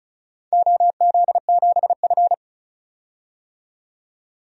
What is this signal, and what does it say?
Morse code, heard as a keyed tone, about 34 wpm, 700 Hz. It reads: OZ7F
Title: DXCC-100 Call Signs - 4 Characters - 35wpm